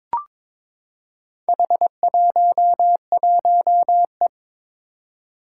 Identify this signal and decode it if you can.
Morse code, heard as a keyed tone, about 22 wpm, 700 Hz. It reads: H11E
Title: Four Challenging Letter-Number Combinations 22wpm